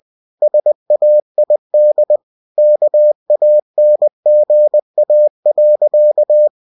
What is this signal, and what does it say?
Morse code, heard as a keyed tone, about 20 wpm, 600 Hz. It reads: SAID KANGA.
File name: cw-books / winnie-the-pooh / winnie-20wpm-page-0072